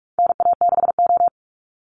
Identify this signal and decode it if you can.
Morse code, heard as a keyed tone, about 44 wpm, 700 Hz. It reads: NA6O
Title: NA SOTA Chaser Callsigns 45wpm